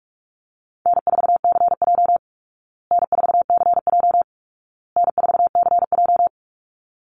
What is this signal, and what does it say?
Morse code, heard as a keyed tone, about 44 wpm, 700 Hz. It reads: N4CJ N4CJ N4CJ
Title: Easy Call Signs Repeated 3x - 45wpm